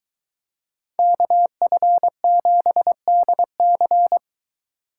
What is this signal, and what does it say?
Morse code, heard as a keyed tone, about 23 wpm, 700 Hz. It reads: KF7DC